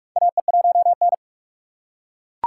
Morse code, heard as a keyed tone, about 41 wpm, 700 Hz. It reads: AE1N
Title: K1USN SST QSO Practice - 45wpm